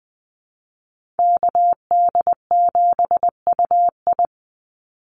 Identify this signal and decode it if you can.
Morse code, heard as a keyed tone, about 20 wpm, 700 Hz. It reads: KD7UI
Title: Call Signs - 20wpm